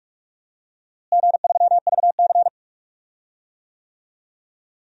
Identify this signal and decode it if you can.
Morse code, heard as a keyed tone, about 45 wpm, 700 Hz. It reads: G3VC
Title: DXCC-100 Call Signs - 4 Characters - Repeated 2x - 45wpm